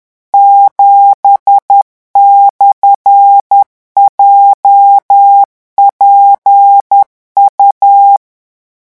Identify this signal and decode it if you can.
Morse code, heard as a keyed tone, about 10 wpm, 800 Hz. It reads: 7/JPU
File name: morse mp3 / revision13-vitesse 10 mots